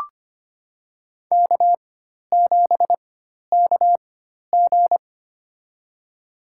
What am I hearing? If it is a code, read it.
Morse code, heard as a keyed tone, about 25 wpm, spaced as for 12 wpm, 700 Hz. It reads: K7KG
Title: USA Call Signs - 4 Characters - 4x Character Spacing - 25wpm